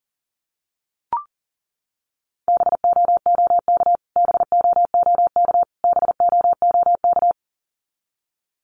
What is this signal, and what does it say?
Morse code, heard as a keyed tone, about 40 wpm, 700 Hz. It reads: BOOK BOOK BOOK
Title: Top 300 Words Repeated 40wpm